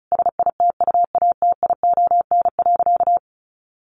Morse code, heard as a keyed tone, about 34 wpm, 700 Hz. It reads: SITUATION.
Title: Sentences from Top 1200 Words 35wpm